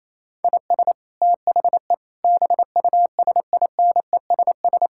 Morse code, heard as a keyed tone, about 28 wpm, 700 Hz. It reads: IS THE BUSINESS
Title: Sentences from Top 1000 Words - Review Part 6 - 28wpm